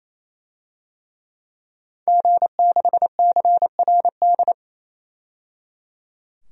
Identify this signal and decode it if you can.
Morse code, heard as a keyed tone, about 28 wpm, 700 Hz. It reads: G6CRD
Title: DXCC-100 Call Signs - 5 Characters - 28wpm